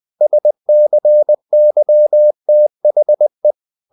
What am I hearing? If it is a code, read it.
Morse code, heard as a keyed tone, about 20 wpm, 600 Hz. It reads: SCYTHE
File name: the old man and the sea / the old man and the sea part 1 20wpm